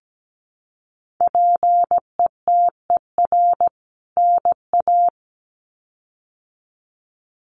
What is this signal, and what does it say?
Morse code, heard as a keyed tone, about 17 wpm, 700 Hz. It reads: PETER NA
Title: CWT Exchange 17wpm